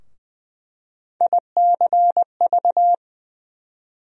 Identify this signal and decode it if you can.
Morse code, heard as a keyed tone, about 20 wpm, 700 Hz. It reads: ICV
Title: USA Call Sign Postfixes - 3 Letters 20wpm